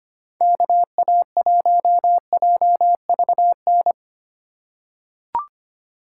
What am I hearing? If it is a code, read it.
Morse code, heard as a keyed tone, about 25 wpm, 700 Hz. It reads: KA1JVN
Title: USA Call Signs - 6 Characters - 2x Character Spacing - 25wpm